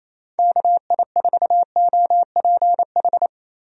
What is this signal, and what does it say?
Morse code, heard as a keyed tone, about 28 wpm, 700 Hz. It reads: KI4OPH
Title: USA Call Signs - 6 Characters - Repeated 2x - 28wpm